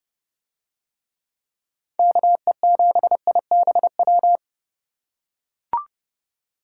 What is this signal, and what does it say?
Morse code, heard as a keyed tone, about 30 wpm, 700 Hz. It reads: KE7IBW